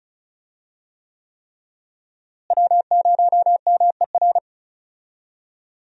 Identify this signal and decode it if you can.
Morse code, heard as a keyed tone, about 36 wpm, 700 Hz. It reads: W0MER
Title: USA Call Signs - 5 Characters - Repeated 3x - 35wpm